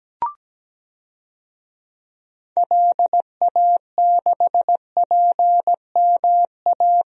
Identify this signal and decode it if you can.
Morse code, heard as a keyed tone, about 17 wpm, 700 Hz. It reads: LA6PMA
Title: DXCC-100 Call Signs - 6 Characters - Repeated 2x - 17wpm